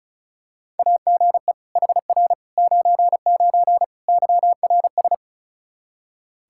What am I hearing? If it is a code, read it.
Morse code, heard as a keyed tone, about 36 wpm, 700 Hz. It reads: AGE HR 99 YRS
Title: Practice QSO - Exchange 3 - More Info - 35wpm